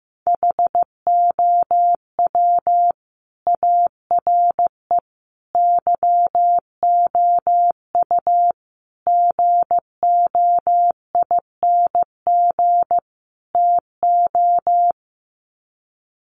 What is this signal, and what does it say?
Morse code, heard as a keyed tone, about 15 wpm, 700 Hz. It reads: HOW ARE YOU GOING TO
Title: Sets of 5 Words 15wpm